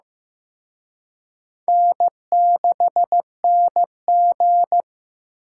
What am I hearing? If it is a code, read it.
Morse code, heard as a keyed tone, about 18 wpm, 700 Hz. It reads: N6NG
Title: Speed Racing - USA 1x2 Call Signs 15wpm